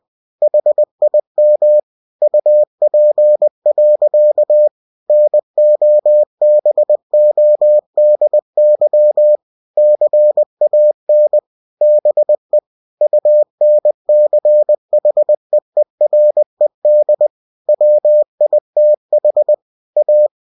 Morse code, heard as a keyed tone, about 20 wpm, 600 Hz. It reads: HIM UP. NOBODY CAN BE UNCHEERED WITH A
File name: cw-books / winnie-the-pooh / winnie-20wpm-page-0049